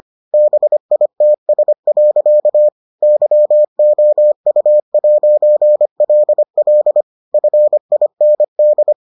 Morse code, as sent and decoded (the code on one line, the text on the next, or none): -... .. - ... .-.-.- / -.-- --- ..- .----. .-.. .-.. / ..-. .. -. -..
BITS. YOU'LL FIND